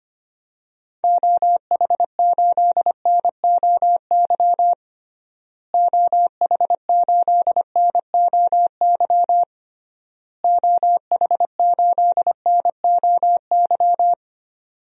Morse code, heard as a keyed tone, about 25 wpm, 700 Hz. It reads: OH8NOY OH8NOY OH8NOY
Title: DXCC-100 Call Signs - 6 Characters - Repeated 3x - 25wpm